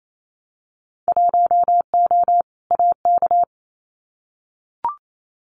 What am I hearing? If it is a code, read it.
Morse code, heard as a keyed tone, about 28 wpm, 690 Hz. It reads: 1O AK
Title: Winter Field Day Exchanges Encore 28wpm